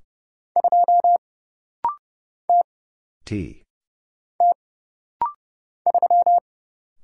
Morse code, heard as a keyed tone, about 31 wpm, 700 Hz. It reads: 2 T T 3
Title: Single Letter-Number Alt - Rapid-Fire 30wpm